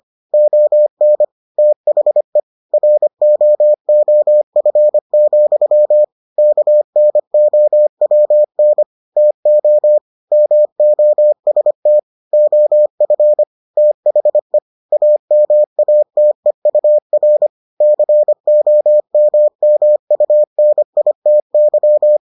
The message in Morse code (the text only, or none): ON THE ROOF, KNOWN TO MOST OF THE AMATEUR COMMUNITY